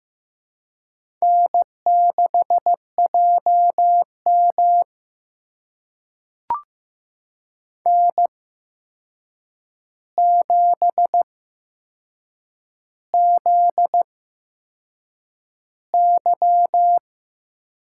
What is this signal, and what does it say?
Morse code, heard as a keyed tone, about 15 wpm, spaced as for 6 wpm, 700 Hz. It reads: N6JM N7ZY
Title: USA Call Signs - 4 Characters - 8x Character Spacing - 15wpm